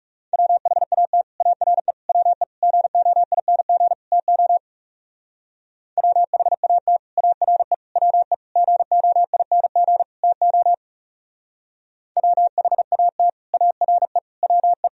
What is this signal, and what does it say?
Morse code, heard as a keyed tone, about 37 wpm, 700 Hz. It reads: WHAT ARE WE GOING TO WHAT ARE WE GOING TO WHAT ARE WE
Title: Speed-Racing - Sets of 5 Words 30wpm